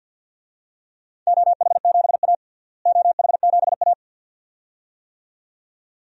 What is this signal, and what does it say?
Morse code, heard as a keyed tone, about 52 wpm, 700 Hz. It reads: OH7A OH7A
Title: DXCC-100 Call Signs - 4 Characters - Repeated 2x - 50wpm